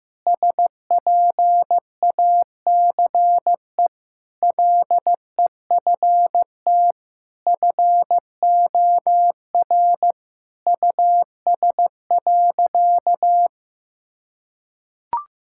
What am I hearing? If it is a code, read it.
Morse code, heard as a keyed tone, about 15 wpm, 700 Hz. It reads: SPACE LEFT FOR US.
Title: Sentences from Top 1000 Words - Review Part 5 - 15wpm